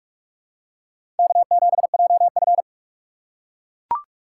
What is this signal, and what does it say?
Morse code, heard as a keyed tone, about 42 wpm, 700 Hz. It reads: K7JF
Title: USA Call Signs - 4 Characters - 2x Character Spacing - 45wpm